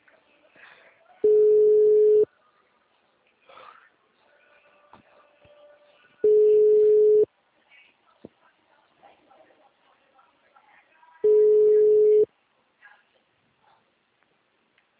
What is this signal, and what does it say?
Ringback tone (Europe, 425 Hz), with speech.